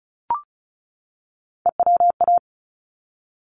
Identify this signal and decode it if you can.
Morse code, heard as a keyed tone, about 34 wpm, 700 Hz. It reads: EWA
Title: ARRL Sections 35wpm